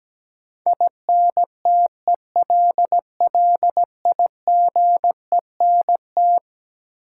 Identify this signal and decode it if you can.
Morse code, heard as a keyed tone, about 17 wpm, 700 Hz. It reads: INTELLIGENT
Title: Sentences from Top 1000 Words - Review Part 7 - New Word Drill - 17wpm